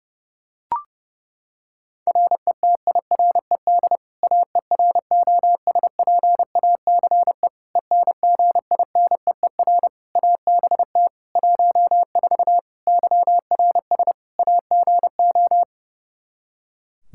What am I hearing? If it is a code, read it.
Morse code, heard as a keyed tone, about 32 wpm, 700 Hz. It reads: RETIRED AEROSPACE ENGINEER ABT 14 YRS AGO
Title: K1USN SST QSO Practice 30wpm